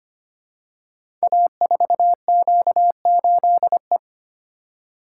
Morse code, heard as a keyed tone, about 25 wpm, 700 Hz. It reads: A4Q8E
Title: Code Groups - 5 Letter:Numbers - 25wpm